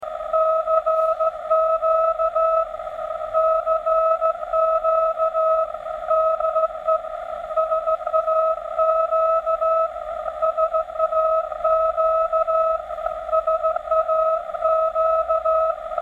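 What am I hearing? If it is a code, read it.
Morse code, heard as a keyed tone, about 15 wpm, 640 Hz, with radio noise. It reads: CQ CQ DE SAQ SAQ SAQ
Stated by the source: SAQ Morse transmission received in the Netherlands on an EF80 regenerative audion receiver with separate oscillator and selective ECL86 audio amplifier.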